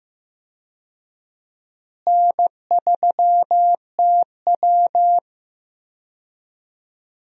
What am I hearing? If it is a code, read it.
Morse code, heard as a keyed tone, about 15 wpm, 700 Hz. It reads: N3TW
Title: USA Call Signs - 4 Characters - Repeated 3x - 15wpm